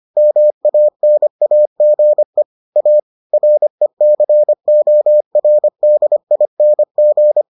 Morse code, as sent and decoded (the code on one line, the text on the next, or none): -- .- -. .- --. / .- / .-. -.-. --- .-. -.. .. -. --.
MANAGE A RECORDING